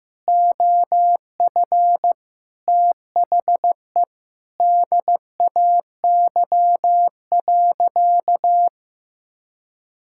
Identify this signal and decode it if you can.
Morse code, heard as a keyed tone, about 15 wpm, 700 Hz. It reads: OF THE DAY.
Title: Sentences from Top 1600 Words 15wpm